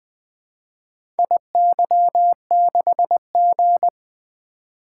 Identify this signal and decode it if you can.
Morse code, heard as a keyed tone, about 20 wpm, 700 Hz. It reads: IY6G
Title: Four Letter-Number Combinations - 20wpm